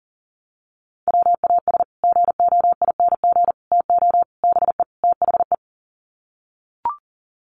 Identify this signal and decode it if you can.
Morse code, heard as a keyed tone, about 40 wpm, 700 Hz. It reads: WAS GOING TO BE THE
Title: Sets of 5 Words Encore2 40wpm